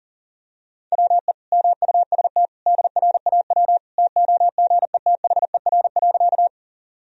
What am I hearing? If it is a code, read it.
Morse code, heard as a keyed tone, about 40 wpm, 700 Hz. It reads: WE MUST DRAW TOGETHER.
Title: Sentences from Top 1000 Words - Review Part 1 - 40wpm